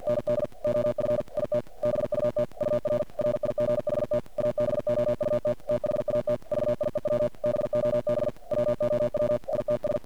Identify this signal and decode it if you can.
Morse code, heard as a keyed tone, about 48 wpm, 590 Hz. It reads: ND OF IT BUT UR RIGHT ABOUT THAT VIEW BOB WOW ITS
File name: archived / CRSnet